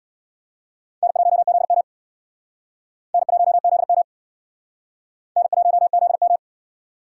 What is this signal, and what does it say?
Morse code, heard as a keyed tone, about 66 wpm, 700 Hz. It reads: N1ZM N1ZM N1ZM